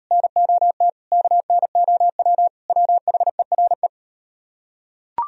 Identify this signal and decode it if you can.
Morse code, heard as a keyed tone, about 38 wpm, 700 Hz. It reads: NOT KNOW WHERE